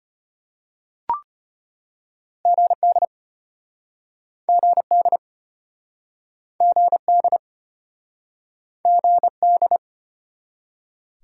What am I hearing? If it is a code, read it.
Morse code, heard as a keyed tone, about 34 wpm, 700 Hz. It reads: GD GD GD GD